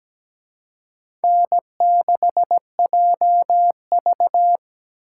Morse code, heard as a keyed tone, about 20 wpm, 700 Hz. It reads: N6JV
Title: Speed Racing - USA 1x2 Call Signs 17wpm